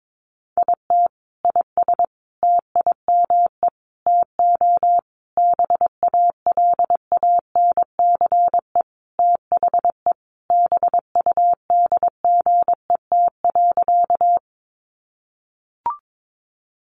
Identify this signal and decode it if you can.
Morse code, heard as a keyed tone, about 22 wpm, 700 Hz. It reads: IT IS TIME TO BALANCE THE BUDGET.
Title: Sentences from Top 1200 Words 22wpm